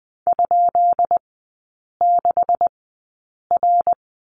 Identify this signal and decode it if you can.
Morse code, heard as a keyed tone, about 20 wpm, spaced as for 8 wpm, 690 Hz. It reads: ?6R
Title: 145 Lesson 24e - Letter y - Three Characters 20wpm